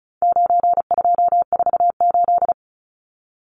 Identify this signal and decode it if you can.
Morse code, heard as a keyed tone, about 34 wpm, 700 Hz. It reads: 9248
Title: Numbers - 4-Digit 35wpm